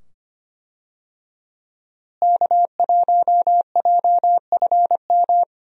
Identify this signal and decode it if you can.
Morse code, heard as a keyed tone, about 25 wpm, 700 Hz. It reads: K1JFM
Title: USA Call Signs - 5 Characters - Repeated 2x - 25wpm